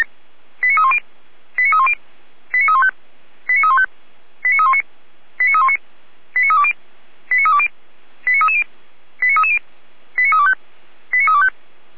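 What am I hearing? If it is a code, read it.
Five-tone selective call (ZVEI), heard as quick runs of five tones, tones of about 70 ms. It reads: 89310 89310 89315 89315 89319 89319 89320 89320 89330 89330 89325 89325
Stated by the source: ZVEI five-tone call received by SDR, fire brigade Orlamünde (feuerwehr-orlamuende).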